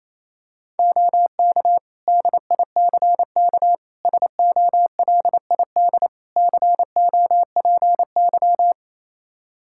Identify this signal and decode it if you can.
Morse code, heard as a keyed tone, about 28 wpm, 700 Hz. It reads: OK DICK SOLID COPY
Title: Practice QSO - Exchange 3 - More Info - 28wpm